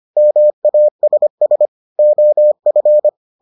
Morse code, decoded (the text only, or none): MASS OF